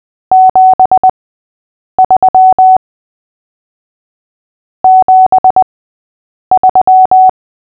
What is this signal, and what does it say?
Morse code, heard as a keyed tone, about 20 wpm, spaced as for 8 wpm, 750 Hz. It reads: 73 73
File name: cw / fists / Short-Story-5-8wpm